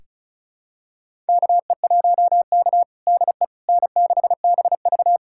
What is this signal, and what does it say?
Morse code, heard as a keyed tone, about 34 wpm, 700 Hz. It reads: KE1K DE N6BV
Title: USA Callsign DE Callsign - 4 Characters - 35wpm